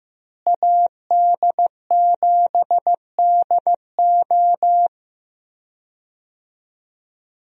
Morse code, heard as a keyed tone, about 15 wpm, 700 Hz. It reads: AD7DO
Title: USA Call Signs - 5 Characters - Repeated 3x - 15wpm